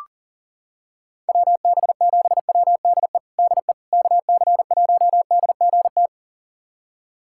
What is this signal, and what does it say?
Morse code, heard as a keyed tone, about 40 wpm, 690 Hz. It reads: WB7WDE DE KC1DGT
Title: USA Callsign DE Callsign - 6 Characters - 40wpm